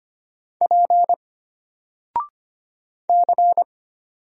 Morse code, heard as a keyed tone, about 25 wpm, 700 Hz. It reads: P C
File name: Single-Letter-Number-25wpm